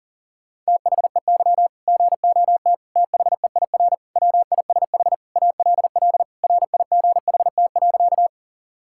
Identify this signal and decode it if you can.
Morse code, heard as a keyed tone, about 40 wpm, 700 Hz. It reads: THEY GOT THEIR WISH ALL RIGHT.
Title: Sentences from Top 1600 Words 40wpm